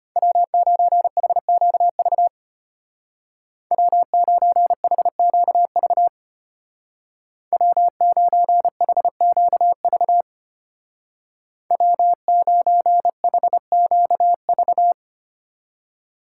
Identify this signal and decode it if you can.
Morse code, heard as a keyed tone, about 30 wpm, 700 Hz. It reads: W9HQV W9HQV W9HQV W9HQV
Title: Speed Racing - USA Call Signs 25wpm